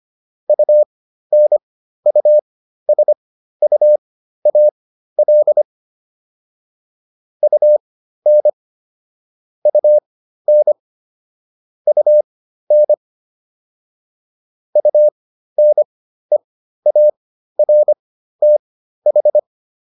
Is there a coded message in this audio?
Yes: Morse code from 0.5 to 19.4 s, heard as a keyed tone, about 25 wpm, spaced as for 13 wpm, 600 Hz. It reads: UNUSUAL UN UN UN UNEARTH